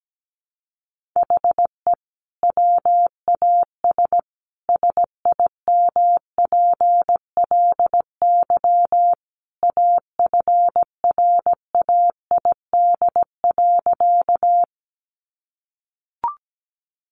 Morse code, heard as a keyed tone, about 17 wpm, 700 Hz. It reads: HE WAS SIMPLY AFRAID.
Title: Sentences from Top 1500 Words 17wpm